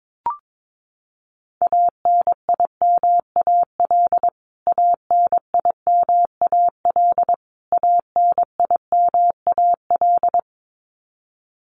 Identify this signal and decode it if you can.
Morse code, heard as a keyed tone, about 22 wpm, 700 Hz. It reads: ANIMAL ANIMAL ANIMAL
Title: Top 300 Words Repeated 22wpm